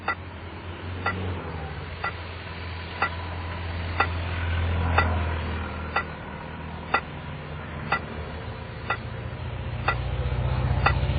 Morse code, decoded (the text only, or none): none